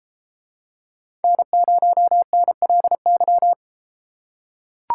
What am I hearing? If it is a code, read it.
Morse code, heard as a keyed tone, about 30 wpm, 700 Hz. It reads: N0NLY